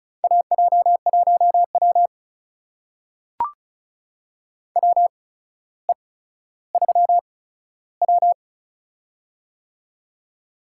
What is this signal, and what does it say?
Morse code, heard as a keyed tone, about 35 wpm, spaced as for 15 wpm, 700 Hz. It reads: AJ1W WE3W
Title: USA Call Signs - 4 Characters - 8x Character Spacing - 35wpm